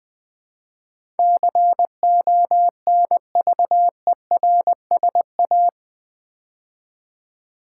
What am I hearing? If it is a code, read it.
Morse code, heard as a keyed tone, about 20 wpm, 700 Hz. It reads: CONVERSA
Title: Build A Word - 4 Letters At A Time - 12-Letter Words - 20wpm